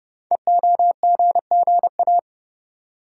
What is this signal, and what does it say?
Morse code, heard as a keyed tone, about 29 wpm, 700 Hz. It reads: EOGGA
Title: Code Groups - 5 Letters - 30wpm